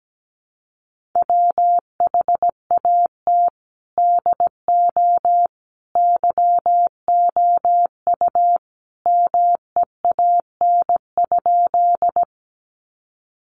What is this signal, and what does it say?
Morse code, heard as a keyed tone, about 17 wpm, 700 Hz. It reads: WHAT DO YOU MEAN?
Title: Sentences from Top 200 Words 17wpm